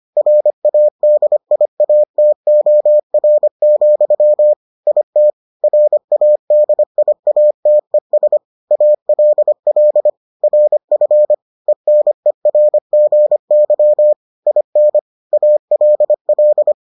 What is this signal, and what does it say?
Morse code, heard as a keyed tone, about 25 wpm, 600 Hz. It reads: RADIATOR, IT RADIATES ALL RF ENERGY IN ALL